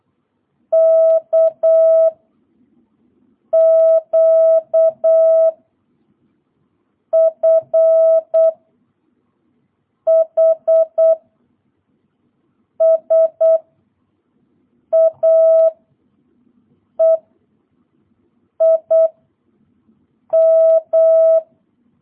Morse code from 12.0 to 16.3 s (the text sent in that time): S A